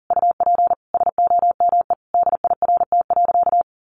Morse code, heard as a keyed tone, about 40 wpm, 700 Hz. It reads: UP SOME DIRT.
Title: A Sound Of Thunder 40wpm